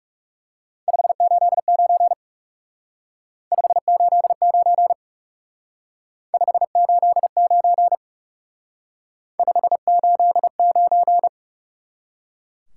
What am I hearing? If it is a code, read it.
Morse code, heard as a keyed tone, about 38 wpm, 700 Hz. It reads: 589 589 589 589